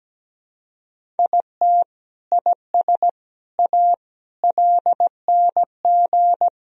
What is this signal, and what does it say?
Morse code, heard as a keyed tone, about 17 wpm, 700 Hz. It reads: IT IS A LNG
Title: Sentences from Top 300 Words and Abbreviations 17wpm